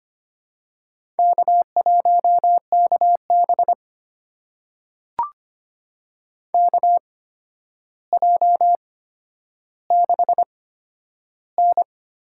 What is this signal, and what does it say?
Morse code, heard as a keyed tone, about 25 wpm, spaced as for 11 wpm, 700 Hz. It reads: K1KB KJ6N
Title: USA Call Signs - 8x Character Spacing - 25wpm